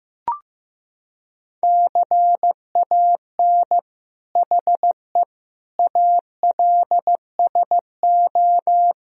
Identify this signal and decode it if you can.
Morse code, heard as a keyed tone, about 16 wpm, 700 Hz. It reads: CAN HE ALSO
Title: Sentences from Top 1000 Words - Review Part 3 - 15wpm